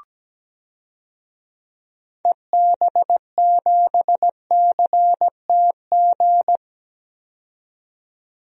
Morse code, heard as a keyed tone, about 17 wpm, 700 Hz. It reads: EB7CTG
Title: DXCC-100 Call Signs - 6 Characters - 17wpm